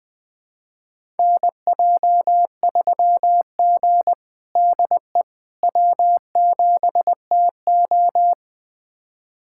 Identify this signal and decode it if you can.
Morse code, heard as a keyed tone, about 21 wpm, 700 Hz. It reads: NJ3G DE W7TO
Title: USA Callsign DE Callsign - 4 Characters - 20wpm